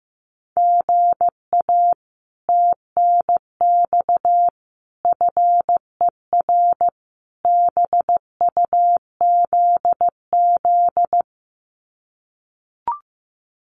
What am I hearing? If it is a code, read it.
Morse code, heard as a keyed tone, about 15 wpm, 700 Hz. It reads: GA TNX FER BUZZ
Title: Common QSO Phrases Encore 15wpm